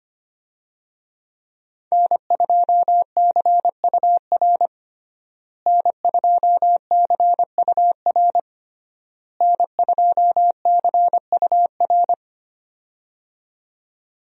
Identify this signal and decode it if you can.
Morse code, heard as a keyed tone, about 25 wpm, 700 Hz. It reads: N2CUR N2CUR N2CUR
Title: USA Call Signs - 5 Characters - Repeated 3x - 25wpm